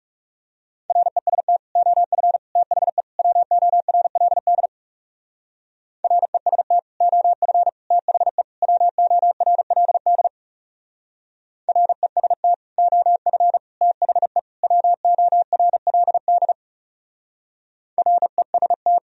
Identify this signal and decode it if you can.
Morse code, heard as a keyed tone, about 39 wpm, 700 Hz. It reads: REST OF THE WORLD REST OF THE WORLD REST OF THE WORLD REST